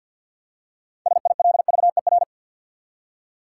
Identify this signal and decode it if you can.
Morse code, heard as a keyed tone, about 52 wpm, 700 Hz. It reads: SILVER
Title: Six Letter Words - 50wpm